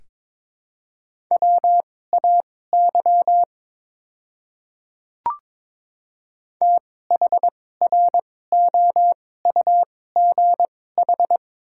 Morse code, heard as a keyed tone, about 22 wpm, spaced as for 16 wpm, 700 Hz. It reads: WAY THROUGH
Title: 2x Character Spacing - Top 300 Words - 22wpm